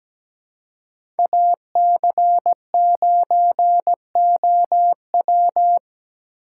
Morse code, heard as a keyed tone, about 17 wpm, 700 Hz. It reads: AC9OW
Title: USA Call Signs - 5 Characters - 4x Character Spacing - 17wpm